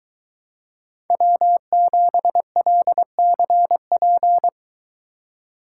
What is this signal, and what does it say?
Morse code, heard as a keyed tone, about 21 wpm, 700 Hz. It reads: W7LCP